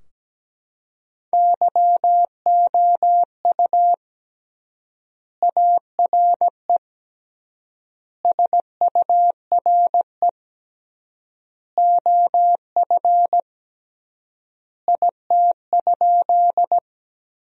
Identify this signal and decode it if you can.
Morse code, heard as a keyed tone, about 17 wpm, 700 Hz. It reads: YOU ARE SURE OF IT?
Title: 3x Word-Spacing - Sentences from Top 300 Words - 17wpm